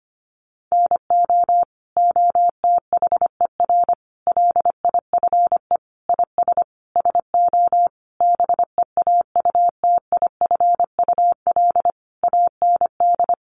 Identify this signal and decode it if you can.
Morse code, heard as a keyed tone, about 25 wpm, 700 Hz. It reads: NO OTHER LIFE IS SO BEAUTIFUL AND
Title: Sentences from Top 1500 Words 25wpm